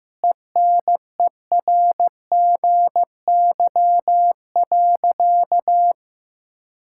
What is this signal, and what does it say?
Morse code, heard as a keyed tone, about 15 wpm, 700 Hz. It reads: ENERGY.
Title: Sentences from Top 1000 Words - Review Part 6 - 15wpm